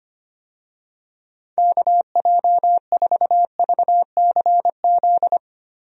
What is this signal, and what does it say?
Morse code, heard as a keyed tone, about 25 wpm, 700 Hz. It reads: KJ4VCZ
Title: USA Call Signs - 6 Characters - Repeated 2x - 25wpm